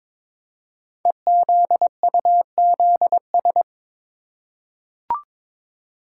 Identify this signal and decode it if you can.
Morse code, heard as a keyed tone, about 22 wpm, 700 Hz. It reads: EZUZS
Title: Code Groups - 5 Letters - 22wpm